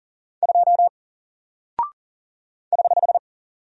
Morse code, heard as a keyed tone, about 40 wpm, 700 Hz. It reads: 2 <HH>
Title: Single Character (Letter:Number:Punctuation:Prosign) Alt 40wpm